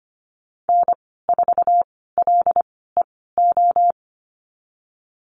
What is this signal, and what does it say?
Morse code, heard as a keyed tone, about 25 wpm, spaced as for 16 wpm, 700 Hz. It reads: N4LEO
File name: Lesson 15g - Number 5 - Call Signs 25wpm